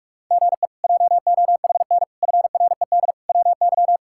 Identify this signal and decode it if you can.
Morse code, heard as a keyed tone, about 46 wpm, 700 Hz. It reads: GE JOHN FRED WY
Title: K1USN SST QSO Practice - 45wpm